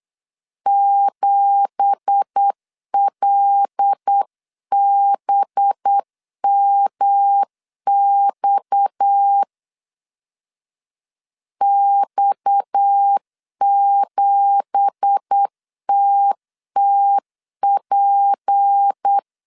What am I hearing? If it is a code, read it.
Morse code, heard as a keyed tone, about 8 wpm, 800 Hz. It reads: 7LBMX X7TTP
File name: CW / DET / 1 9 1500C S B 8wpm 5624